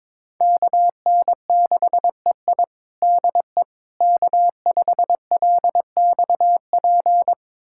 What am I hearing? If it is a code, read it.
Morse code, heard as a keyed tone, about 21 wpm, 700 Hz. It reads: KN6EI DE K5LXP
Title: USA Callsign DE Callsign - 5 Characters - 22wpm